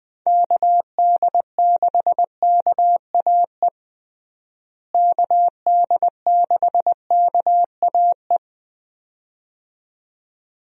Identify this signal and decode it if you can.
Morse code, heard as a keyed tone, about 21 wpm, 700 Hz. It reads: KD6KAE KD6KAE
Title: USA Call Signs - 6 Characters - Repeated 2x - 20wpm